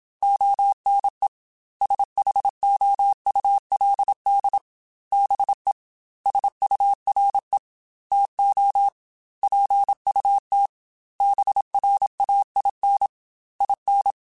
Morse code, heard as a keyed tone, about 27 wpm, 800 Hz. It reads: ONE SHOULD BE SURE TO PUT BRAIN IN
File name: Chuckles / Chuckles-mp3s / BrainMouth @25